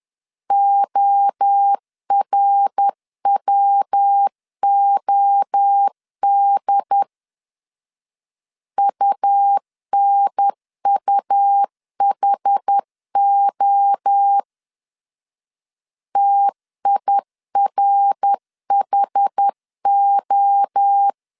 Morse code, heard as a keyed tone, about 10 wpm, 800 Hz. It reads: ORWOD UNUHO TIRHO
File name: CW / DET / 1 3 1500C S B 10wpm 3335